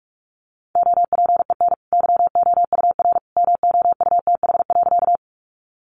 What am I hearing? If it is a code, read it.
Morse code, heard as a keyed tone, about 44 wpm, 700 Hz. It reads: OPEN YOUR MOUTH.
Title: Sentences from Top 1500 Words 45wpm